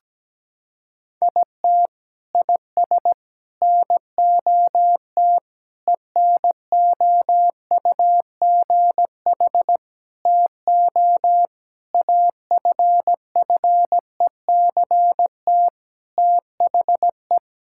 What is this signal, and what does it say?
Morse code, heard as a keyed tone, about 17 wpm, 700 Hz. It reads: IT IS NOT ENOUGH TO AFFECT THE
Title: Sentences from Top 1000 Words - Review Part 6 - 17wpm